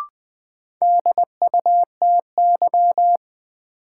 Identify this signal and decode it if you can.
Morse code, heard as a keyed tone, about 20 wpm, 700 Hz. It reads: DUTY
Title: Sentences from Top 1500 Words - New Word Drill - 20wpm